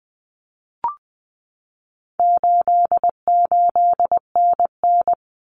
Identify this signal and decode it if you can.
Morse code, heard as a keyed tone, about 20 wpm, 700 Hz. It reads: 88NN
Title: Common Cut Numbers - 4-Digit 20wpm